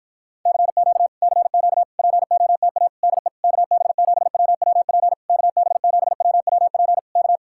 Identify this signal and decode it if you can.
Morse code, heard as a keyed tone, about 52 wpm, 700 Hz. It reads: CQ CQ POTA DE KB7WWP KB7WWP K